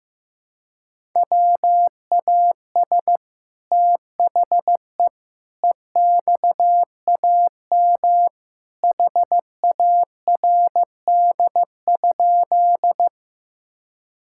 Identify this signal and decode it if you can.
Morse code, heard as a keyed tone, about 15 wpm, 700 Hz. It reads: WAS THE EXAM HARD?
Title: Sentences from Top 1000 Words - Review Part 5 - 15wpm